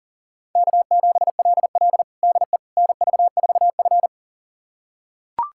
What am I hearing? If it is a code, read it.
Morse code, heard as a keyed tone, about 40 wpm, 700 Hz. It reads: K7LL DE NV4F
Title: USA Callsign DE Callsign - 4 Characters - 40wpm